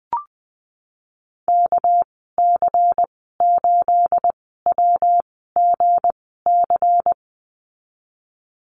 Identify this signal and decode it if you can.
Morse code, heard as a keyed tone, about 20 wpm, spaced as for 15 wpm, 700 Hz. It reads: KC8WGC
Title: 213 Lesson 34g - Number 8 - Call Signs 20wpm